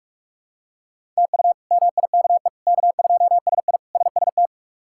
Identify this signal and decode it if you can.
Morse code, heard as a keyed tone, about 47 wpm, 700 Hz. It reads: TU MIKE K2SI SST
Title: K1USN SST QSO Practice - 45wpm